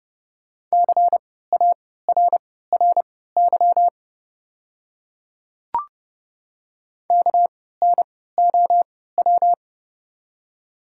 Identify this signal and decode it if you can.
Morse code, heard as a keyed tone, about 31 wpm, spaced as for 17 wpm, 700 Hz. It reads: CARRY KNOW
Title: 3x Character Spacing - Top 500 Words - 30wpm